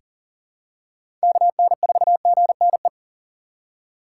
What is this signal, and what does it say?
Morse code, heard as a keyed tone, about 40 wpm, 700 Hz. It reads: KN4GNE
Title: USA Call Signs - 6x Character Spacing - 40wpm